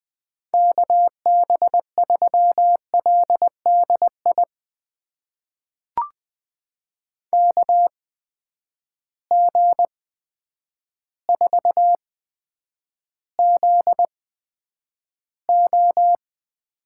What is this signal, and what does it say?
Morse code, heard as a keyed tone, about 20 wpm, spaced as for 9 wpm, 700 Hz. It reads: KB3LDI KG4ZO
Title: USA Call Signs - 6 Characters - 8x Character Spacing - 20wpm